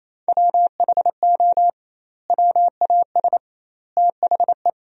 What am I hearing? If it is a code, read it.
Morse code, heard as a keyed tone, about 28 wpm, 700 Hz. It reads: WHO WAS THE
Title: 2x Word-Spacing - Sentences from Top 300 Words - 28wpm